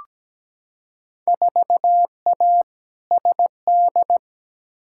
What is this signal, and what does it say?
Morse code, heard as a keyed tone, about 17 wpm, 700 Hz. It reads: 4A SD
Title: Field Day Exchanges - 17wpm